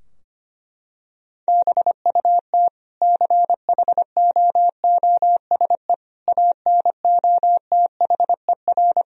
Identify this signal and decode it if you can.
Morse code, heard as a keyed tone, about 25 wpm, 700 Hz. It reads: BUT CHOOSE ANOTHER
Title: Sentences from Top 1000 Words - Review Part 4 - 25wpm